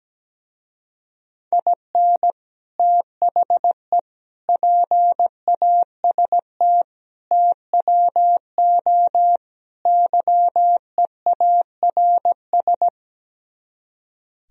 Morse code, heard as a keyed tone, about 17 wpm, 700 Hz. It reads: IN THE PAST TWO YEARS
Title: Speed-Racing - Sets of 5 Words 17wpm